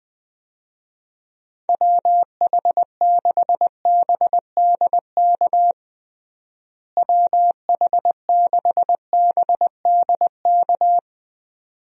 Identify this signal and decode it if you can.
Morse code, heard as a keyed tone, about 20 wpm, 700 Hz. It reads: WH6BDK WH6BDK
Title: USA Call Signs - 6 Characters - Repeated 2x - 20wpm